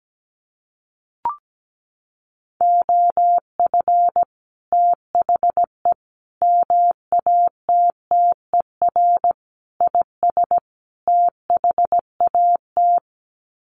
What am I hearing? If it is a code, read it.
Morse code, heard as a keyed tone, about 17 wpm, 700 Hz. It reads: OF THE MATTER IS THAT
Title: Sets of 5 Words Encore 17wpm